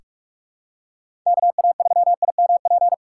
Morse code, heard as a keyed tone, about 42 wpm, 700 Hz. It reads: KA3IMP
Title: USA Call Signs - 6 Characters - 4x Character Spacing - 45wpm